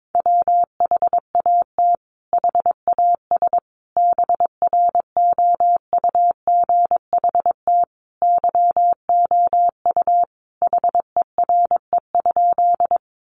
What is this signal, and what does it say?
Morse code, heard as a keyed tone, about 22 wpm, 700 Hz. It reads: WHAT HAS BROUGHT YOU HERE?
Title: Sentences from Top 500 Words 22wpm